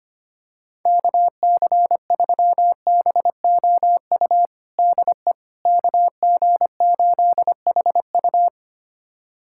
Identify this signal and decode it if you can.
Morse code, heard as a keyed tone, about 25 wpm, 700 Hz. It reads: KC3BOU DE KG8HU
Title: USA Callsign DE Callsign 25wpm